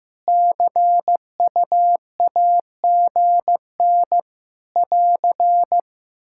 Morse code, heard as a keyed tone, about 15 wpm, 700 Hz. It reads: CUAGN <AR>
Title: Practice QSO - Exchange 4 - Ending - 15wpm